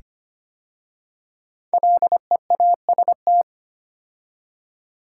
Morse code, heard as a keyed tone, about 25 wpm, 700 Hz. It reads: LEAST
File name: 5 Letter Words - First 3 Letters Given - 25wpm